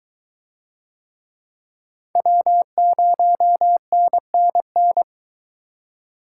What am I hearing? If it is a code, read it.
Morse code, heard as a keyed tone, about 23 wpm, 700 Hz. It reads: W0NNN